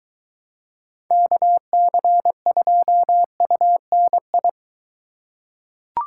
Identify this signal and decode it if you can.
Morse code, heard as a keyed tone, about 22 wpm, 700 Hz. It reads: KC2UNI